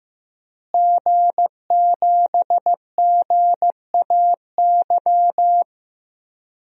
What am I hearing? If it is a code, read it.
Morse code, heard as a keyed tone, about 15 wpm, 700 Hz. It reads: G7GAY
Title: DXCC-100 Call Signs - 5 Characters - Repeated 3x - 15wpm